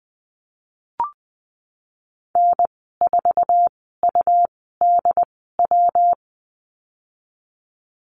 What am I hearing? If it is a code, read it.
Morse code, heard as a keyed tone, about 20 wpm, spaced as for 15 wpm, 700 Hz. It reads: N4UDW
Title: 179 Lesson 29g - Slash - Call Signs 20wpm